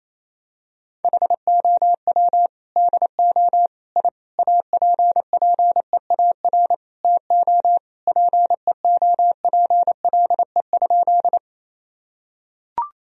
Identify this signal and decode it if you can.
Morse code, heard as a keyed tone, about 27 wpm, 700 Hz. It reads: HOW DO I APPEAR TO PEOPLE?
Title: Sentences from Top 1000 Words - Review Part 2 - 28wpm